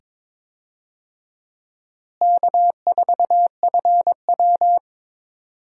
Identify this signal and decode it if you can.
Morse code, heard as a keyed tone, about 22 wpm, 700 Hz. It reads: K4FW
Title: USA Call Signs - 4 Characters - Repeated 2x - 22wpm